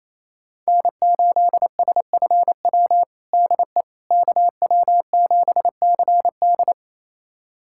Morse code, heard as a keyed tone, about 28 wpm, 700 Hz. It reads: N8SFW DE KW7CD
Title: USA Callsign DE Callsign - 5 Characters - 28wpm